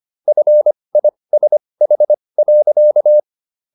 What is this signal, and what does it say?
Morse code, heard as a keyed tone, about 25 wpm, spaced as for 20 wpm, 600 Hz. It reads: FISH.